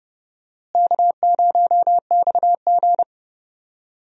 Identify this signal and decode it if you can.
Morse code, heard as a keyed tone, about 30 wpm, 700 Hz. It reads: K0XG
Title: USA Call Signs - 4 Characters - Repeated 3x - 30wpm